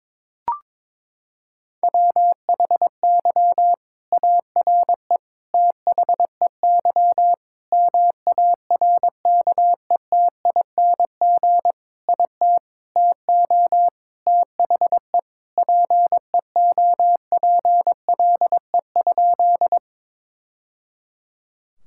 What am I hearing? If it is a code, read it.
Morse code, heard as a keyed tone, about 22 wpm, 700 Hz. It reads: WHY ARE THEY MARKETING IT TO THE PEOPLE?
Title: Sentences from Top 1000 Words - Review Part 5 - 22wpm